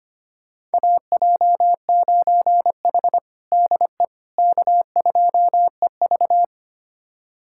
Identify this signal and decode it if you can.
Morse code, heard as a keyed tone, about 25 wpm, 700 Hz. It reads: AJ9H DE K2EV
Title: USA Callsign DE Callsign - 4 Characters - 25wpm